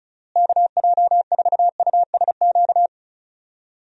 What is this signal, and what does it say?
Morse code, heard as a keyed tone, about 36 wpm, 700 Hz. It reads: KJ4USQ
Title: USA Call Signs - 6 Characters - Repeated 2x - 35wpm